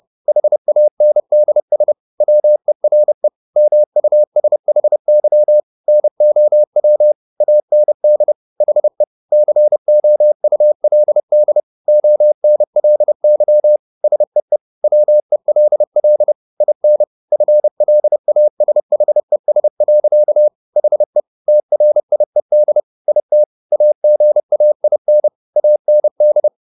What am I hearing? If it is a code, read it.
Morse code, heard as a keyed tone, about 30 wpm, 600 Hz. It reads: HANDS WERE MUSHY NOW AND HE COULD ONLY SEE WELL IN FLASHES. HE TRIED IT AGAIN AND